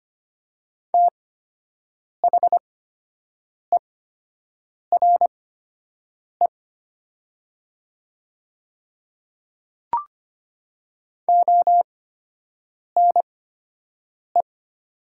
Morse code, heard as a keyed tone, about 25 wpm, spaced as for 7 wpm, 700 Hz. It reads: THERE ONE
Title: 8x Character Spacing - Top 200 Words - 25wpm